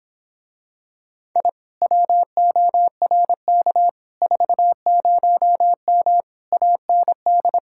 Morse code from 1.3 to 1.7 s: I

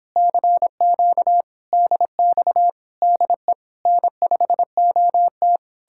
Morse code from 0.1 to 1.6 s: CQ

